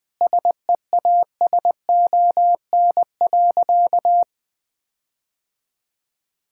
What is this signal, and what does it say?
Morse code, heard as a keyed tone, about 20 wpm, 700 Hz. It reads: SEASON.
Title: Sentences with 7-Word Phrases - 20wpm